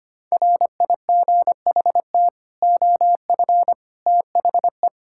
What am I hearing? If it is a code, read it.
Morse code, heard as a keyed tone, about 25 wpm, 700 Hz. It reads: RIGHT OF THE
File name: Sets of 9 Words - 25wpm